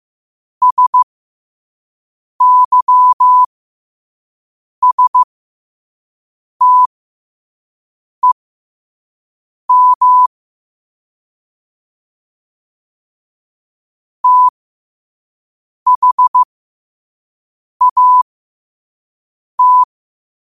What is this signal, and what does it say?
Morse code, heard as a keyed tone, about 15 wpm, spaced as for 5 wpm, 1000 Hz. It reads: SYSTEM THAT
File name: qotdfiles / 5wpm